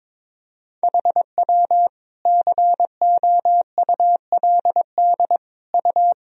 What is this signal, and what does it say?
Morse code, heard as a keyed tone, about 22 wpm, 700 Hz. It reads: HW COULD U
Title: Sentences from Top 300 Words and Abbreviations 22wpm